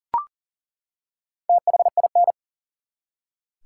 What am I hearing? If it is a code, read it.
Morse code, heard as a keyed tone, about 40 wpm, 700 Hz. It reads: THIN
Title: Four Letter Words - 40wpm